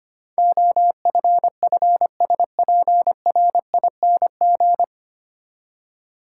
Morse code, heard as a keyed tone, about 25 wpm, 690 Hz. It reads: OFFSPRING
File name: practice / 25wpm / 9 Letter Words - First 3 Letters Given - 25wpm